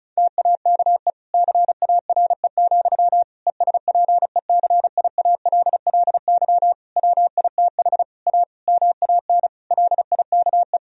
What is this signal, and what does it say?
Morse code, heard as a keyed tone, about 36 wpm, 700 Hz. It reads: TAKE CARE, ESPECIALLY WITH A MAN LIKE
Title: Sentences from Top 1000 Words - Review Part 4 - 35wpm